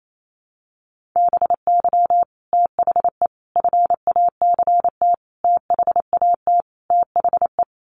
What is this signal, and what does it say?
Morse code, heard as a keyed tone, about 28 wpm, 700 Hz. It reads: BY THE FACT THAT THE
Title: Sets of 5 Words Encore 28wpm